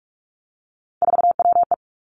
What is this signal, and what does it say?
Morse code, heard as a keyed tone, about 46 wpm, 700 Hz. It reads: 4WE
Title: Three Challenging Letter-Number Combinations 45wpm